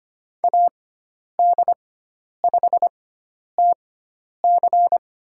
Morse code, heard as a keyed tone, about 25 wpm, spaced as for 10 wpm, 700 Hz. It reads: AD5TC
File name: Lesson 16g - Letter c - Call Signs 25wpm